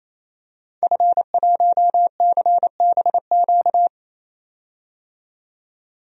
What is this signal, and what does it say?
Morse code, heard as a keyed tone, about 28 wpm, 700 Hz. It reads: F1CBQ
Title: DXCC-100 Call Signs - 5 Characters - 28wpm